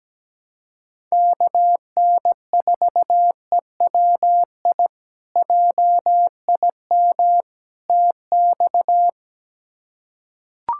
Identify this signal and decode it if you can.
Morse code, heard as a keyed tone, about 17 wpm, 700 Hz. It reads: KN4EWI JIM TX
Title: K1USN SST QSO Practice - 17wpm